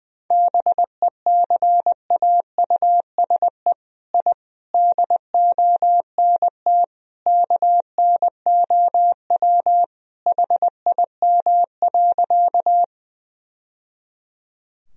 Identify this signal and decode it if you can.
Morse code, heard as a keyed tone, about 20 wpm, 700 Hz. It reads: BECAUSE I DONT KNOW HIM.
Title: Sentences from Top 1000 Words - Review Part 3 - 20wpm